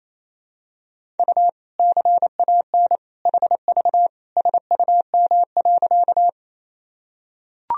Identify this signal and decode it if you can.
Morse code, heard as a keyed tone, about 28 wpm, 700 Hz. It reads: U CAN HV SUM.
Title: Sentences from Top 100 Words and Abbreviations 28wpm